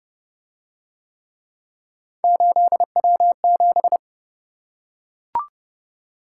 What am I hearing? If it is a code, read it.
Morse code, heard as a keyed tone, about 27 wpm, 700 Hz. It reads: 8W7